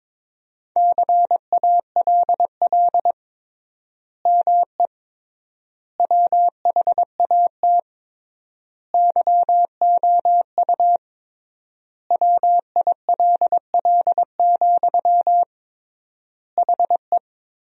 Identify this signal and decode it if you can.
Morse code, heard as a keyed tone, about 22 wpm, 700 Hz. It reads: CALL ME WHAT YOU WILL, HE
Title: 3x Word-Spacing - Sentences from Top 200 Words - 22wpm